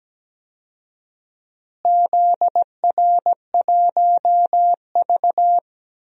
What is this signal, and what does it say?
Morse code, heard as a keyed tone, about 17 wpm, 700 Hz. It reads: ZR1V
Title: DXCC-100 Call Signs - 4 Characters - Repeated 3x - 17wpm